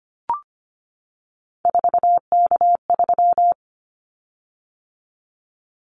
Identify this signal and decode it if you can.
Morse code, heard as a keyed tone, about 25 wpm, 700 Hz. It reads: 4K3
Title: Callsign Prefixes for Top 100 DXCC - 25wpm